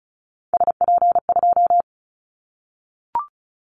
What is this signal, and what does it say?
Morse code, heard as a keyed tone, about 34 wpm, 700 Hz. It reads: SP2
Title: Callsign Prefixes for Top 25 DXCC - 35wpm